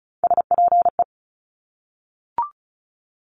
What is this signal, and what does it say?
Morse code, heard as a keyed tone, about 36 wpm, 700 Hz. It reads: SPE
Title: Top Three Letter Combinations in Words 35wpm